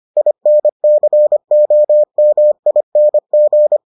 Morse code, heard as a keyed tone, about 25 wpm, 600 Hz. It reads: INCOMING